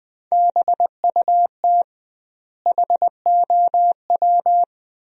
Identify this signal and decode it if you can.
Morse code, heard as a keyed tone, about 20 wpm, 700 Hz. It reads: BUT HOW
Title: 2x Word-Spacing - Sentences from Top 100 Words - 20wpm